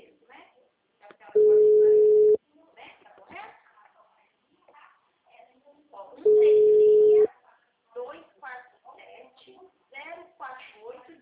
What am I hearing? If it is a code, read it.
Special information tone, with speech.
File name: out- -2025- - - .45451